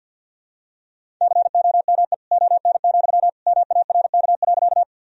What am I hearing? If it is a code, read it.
Morse code, heard as a keyed tone, about 51 wpm, 700 Hz. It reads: COME ON, MARK.
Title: Sentences from Top 1000 Words - Review Part 1 - 50wpm